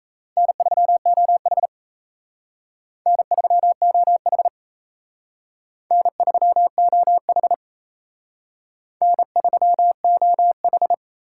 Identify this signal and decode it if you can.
Morse code, heard as a keyed tone, about 35 wpm, 700 Hz. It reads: N3OH N3OH N3OH N3OH
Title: Speed Racing - USA 1x2 Call Signs 28wpm